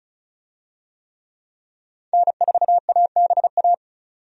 Morse code, heard as a keyed tone, about 36 wpm, 700 Hz. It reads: N4ABA
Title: USA Call Signs - 5 Characters - 35wpm